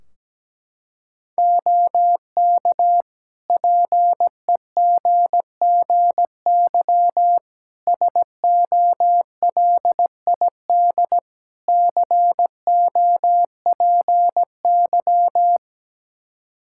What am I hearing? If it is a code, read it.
Morse code, heard as a keyed tone, about 17 wpm, 700 Hz. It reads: OK PEGGY SOLID COPY
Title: Practice QSO - Exchange 3 - More Info - 17wpm